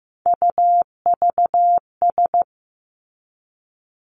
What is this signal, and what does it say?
Morse code, heard as a keyed tone, about 15 wpm, 700 Hz. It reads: UVS
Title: Three Challenging Letter-Number Combinations 15wpm